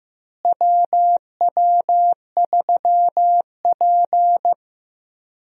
Morse code, heard as a keyed tone, about 15 wpm, 700 Hz. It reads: WW3P
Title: USA Call Signs - 4 Characters - Repeated 3x - 15wpm